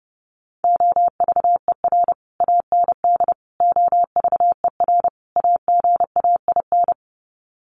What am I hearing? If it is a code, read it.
Morse code, heard as a keyed tone, about 30 wpm, 700 Hz. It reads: OVER AND OVER AGAIN
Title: Sets of 4 Words 30wpm